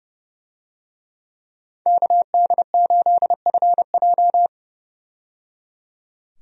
Morse code, heard as a keyed tone, about 30 wpm, 700 Hz. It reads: KD8FJ
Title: USA Call Signs - 5 Characters - 30wpm